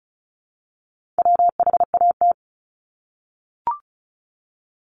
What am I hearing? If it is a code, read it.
Morse code, heard as a keyed tone, about 36 wpm, 700 Hz. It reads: WHAT
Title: Multi-letter Phonemes 35wpm